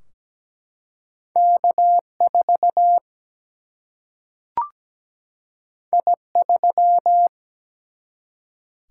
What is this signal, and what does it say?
Morse code, heard as a keyed tone, about 17 wpm, 700 Hz. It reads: K4 I3
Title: Single-Letter Single-Number 17wpm